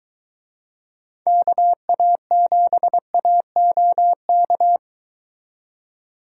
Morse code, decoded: KA7AOK

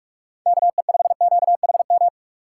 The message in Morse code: KE5QHM

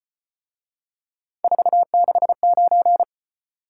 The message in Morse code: 469

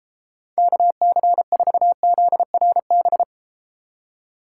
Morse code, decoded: KC4ZRB

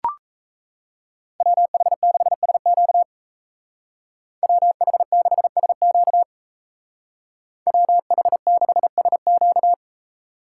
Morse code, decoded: WH6SQ WH6SQ WH6SQ

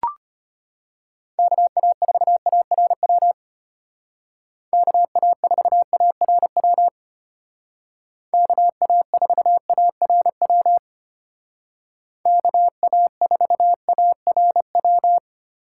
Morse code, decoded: KA4ARW KA4ARW KA4ARW KA4ARW

